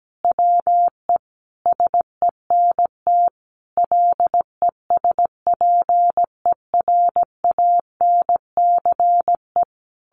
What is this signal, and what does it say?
Morse code, heard as a keyed tone, about 17 wpm, 700 Hz. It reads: WE SENT LESPERANCE
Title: A Sound Of Thunder 17wpm